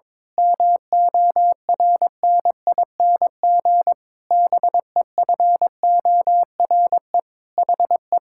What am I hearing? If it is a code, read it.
Morse code, heard as a keyed tone, about 22 wpm, 700 Hz. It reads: MORNING BEFORE HE
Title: Pangram Challenge v2 - 22wpm